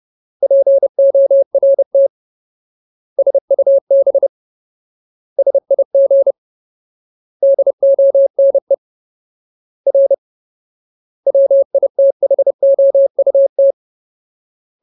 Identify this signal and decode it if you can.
Morse code, heard as a keyed tone, about 30 wpm, 550 Hz. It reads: PORT SUB SIG DONE R WITHOUT
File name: Ultimate Word List / 30WPM Ultimate